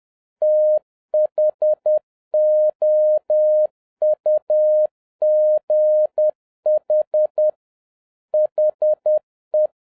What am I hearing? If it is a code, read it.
Morse code, heard as a keyed tone, about 10 wpm, 610 Hz. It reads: THOUGH HE
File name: 2011-07-mp3 / n36 THE 3.11 2 10wpm